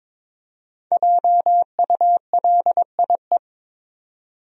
Morse code, heard as a keyed tone, about 23 wpm, 700 Hz. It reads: JULIE
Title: USA Common Names - Max 5 Letters - 22wpm